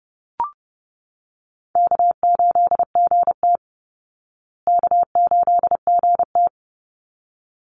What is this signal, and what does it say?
Morse code, heard as a keyed tone, about 30 wpm, 700 Hz. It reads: K8GT K8GT
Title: Easy Call Signs Repeated 2x - 30wpm